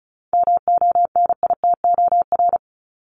Morse code, heard as a keyed tone, about 34 wpm, 700 Hz. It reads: MONITOR
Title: Top 1751-2000 Words 35wpm